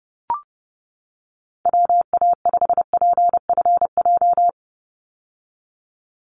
Morse code, heard as a keyed tone, about 30 wpm, 700 Hz. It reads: WA5PFJ
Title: Call Signs - 30wpm